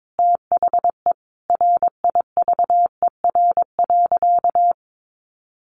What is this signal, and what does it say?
Morse code, heard as a keyed tone, about 22 wpm, 700 Hz. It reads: THE RIVER.
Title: Sentences from Top 1200 Words 22wpm